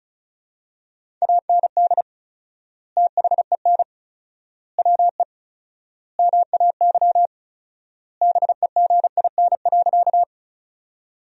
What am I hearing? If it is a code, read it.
Morse code, heard as a keyed tone, about 36 wpm, 700 Hz. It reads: AND THEN WE MAY BEGIN.
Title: 4x Word-Spacing - Sentences from Top 300 Words - 35wpm